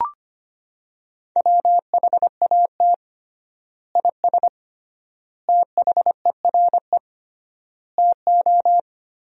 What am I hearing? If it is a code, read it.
Morse code, heard as a keyed tone, about 25 wpm, 700 Hz. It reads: WHAT IS THERE TO
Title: 3x Word-Spacing - Sentences from Top 100 Words - 25wpm